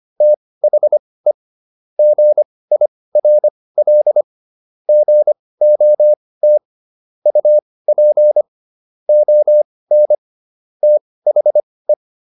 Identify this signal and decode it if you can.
Morse code, heard as a keyed tone, about 25 wpm, spaced as for 18 wpm, 600 Hz. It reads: THE GIRL GOT UP ON THE